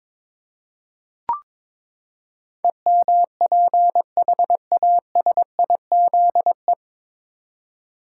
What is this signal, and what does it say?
Morse code, heard as a keyed tone, about 22 wpm, 700 Hz. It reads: EMPHASIZE
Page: Morse Code Ninja practice 22wpm